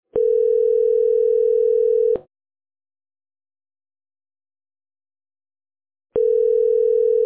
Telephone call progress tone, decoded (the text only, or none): ringback tone